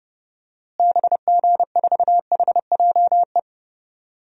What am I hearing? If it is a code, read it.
Morse code, heard as a keyed tone, about 30 wpm, 700 Hz. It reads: BG4HJE
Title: DXCC-100 Call Signs - 6 Characters - Repeated 3x - 30wpm